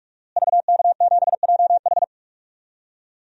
Morse code, heard as a keyed tone, about 46 wpm, 700 Hz. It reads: VK7JH
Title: DXCC-100 Call Signs - 5 Characters - Repeated 2x - 45wpm